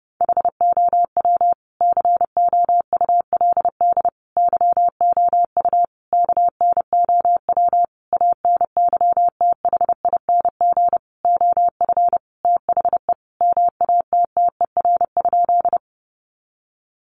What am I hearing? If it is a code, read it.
Morse code, heard as a keyed tone, about 30 wpm, 700 Hz. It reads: HOW COULD YOU KNOW ANYTHING OF THE MATTER?
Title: Sentences from Top 1000 Words 30wpm